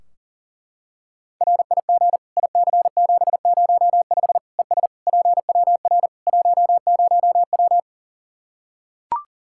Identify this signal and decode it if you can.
Morse code, heard as a keyed tone, about 37 wpm, 700 Hz. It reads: RIG IC705 ES PWR 10W
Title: Practice QSO - Exchange 2 - Info - 40wpm